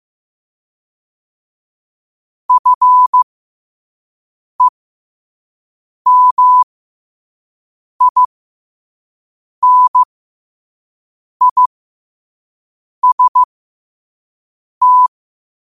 Morse code, heard as a keyed tone, about 15 wpm, spaced as for 5 wpm, 1000 Hz. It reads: FEMINIST
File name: qotdfiles / 5wpm